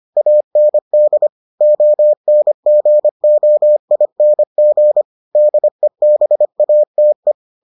Morse code, heard as a keyed tone, about 25 wpm, 600 Hz. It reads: AND ONGOING DEBATE